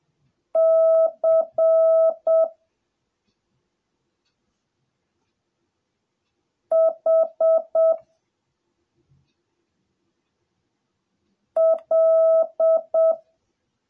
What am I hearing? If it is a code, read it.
Morse code, heard as a keyed tone, about 7 wpm, 650 Hz. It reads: C H L